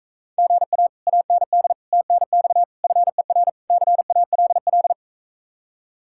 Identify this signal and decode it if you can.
Morse code, heard as a keyed tone, about 45 wpm, 700 Hz. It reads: GA AND TNX FER CALL